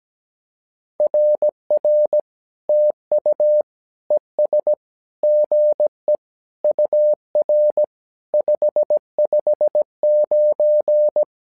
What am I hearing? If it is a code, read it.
Morse code, heard as a keyed tone, about 17 wpm, 600 Hz. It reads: RR TU ES GE UR 559
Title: SOTA QSO Practice 17wpm